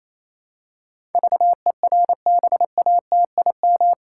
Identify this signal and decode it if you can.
Morse code, heard as a keyed tone, about 28 wpm, 700 Hz. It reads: VERBATIM
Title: Pangram Challenge v2 - 28wpm